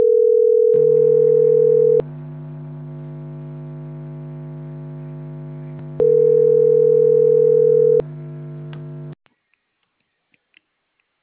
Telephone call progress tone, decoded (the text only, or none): ringback tone